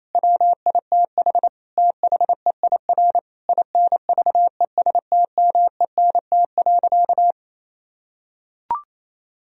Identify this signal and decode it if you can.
Morse code, heard as a keyed tone, about 28 wpm, 700 Hz. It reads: WITH THEIR INVESTMENT.
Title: Sentences from Top 1000 Words - Review Part 4 - 28wpm